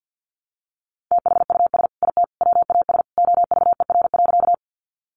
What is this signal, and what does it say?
Morse code, heard as a keyed tone, about 50 wpm, 700 Hz. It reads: THUS IT WAS OVER.
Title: Sentences from Top 800 Words 50wpm